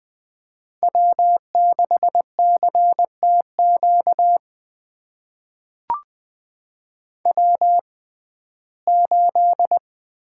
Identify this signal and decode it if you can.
Morse code, heard as a keyed tone, about 20 wpm, spaced as for 14 wpm, 700 Hz. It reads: W6CTQ W8
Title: USA Call Signs - 5 Characters - 6x Character Spacing - 20wpm